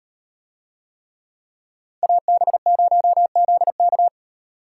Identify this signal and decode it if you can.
Morse code, heard as a keyed tone, about 38 wpm, 700 Hz. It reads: AB0ZK